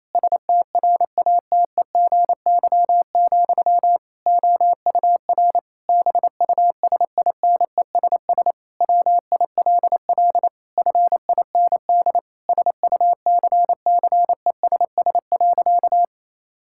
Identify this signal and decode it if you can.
Morse code, heard as a keyed tone, about 28 wpm, 700 Hz. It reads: STRATEGY, OUR BUSINESS WILL FIND SUCCESS.
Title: Sentences from Top 1000 Words - Review Part 5 - 28wpm